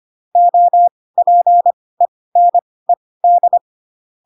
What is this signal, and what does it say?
Morse code, heard as a keyed tone, about 25 wpm, spaced as for 18 wpm, 700 Hz. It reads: OPENED
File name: Pinocchio / Pinocchio ch6 12 25 ik1hge